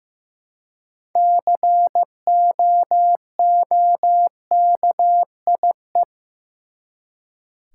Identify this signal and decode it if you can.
Morse code, heard as a keyed tone, about 15 wpm, 700 Hz. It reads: COOKIE
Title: Six Letter Words - 15wpm